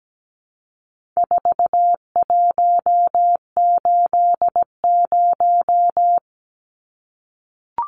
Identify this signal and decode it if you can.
Morse code, heard as a keyed tone, about 17 wpm, 700 Hz. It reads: 4180
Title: Numbers - 4-Digit 17wpm